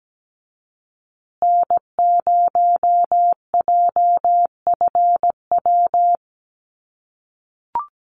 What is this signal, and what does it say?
Morse code, heard as a keyed tone, about 17 wpm, 700 Hz. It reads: N0JFW
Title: USA Call Signs Repeated 2x 17wpm